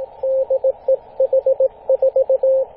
Morse code, heard as a keyed tone, about 17 wpm, 530 Hz, with radio noise. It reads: DE H4
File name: posts / 2016-may-09-0841z-h44lg-14-013mhz-cw